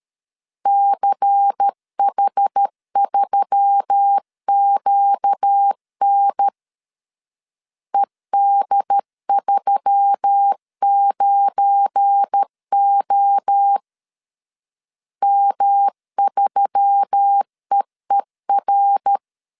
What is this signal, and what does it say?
Morse code, heard as a keyed tone, about 13 wpm, 800 Hz. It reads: CH3QN ED39O M3EER
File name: CW / DET / 1 9 1500C S A 12wpm 3800